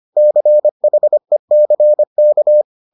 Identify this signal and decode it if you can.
Morse code, heard as a keyed tone, about 25 wpm, 600 Hz. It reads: CHECK